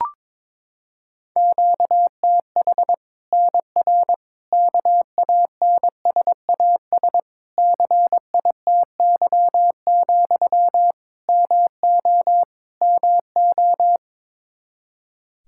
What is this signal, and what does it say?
Morse code, heard as a keyed tone, about 21 wpm, 700 Hz. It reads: QTH NR KANSAS CITY, MO MO
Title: Practice QSO - Exchange 1 - Beginning - 22wpm